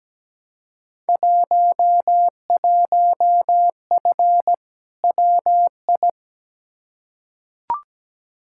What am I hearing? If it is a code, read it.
Morse code, heard as a keyed tone, about 17 wpm, 700 Hz. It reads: 11F WI
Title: Field Day Exchanges Encore - 17wpm